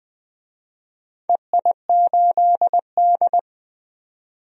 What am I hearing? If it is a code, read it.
Morse code, heard as a keyed tone, about 20 wpm, 700 Hz. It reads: EI8D
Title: DXCC-100 Call Signs - 4 Characters - 20wpm